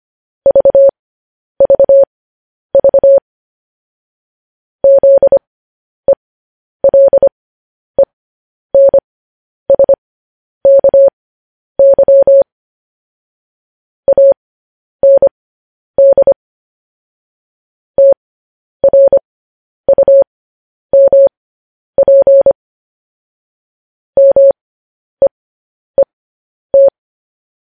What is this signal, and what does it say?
Morse code, heard as a keyed tone, about 25 wpm, spaced as for 10 wpm, 550 Hz. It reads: VVV ZELENSKY AND TRUMP MEET
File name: podcast / News headlines-1x-25x10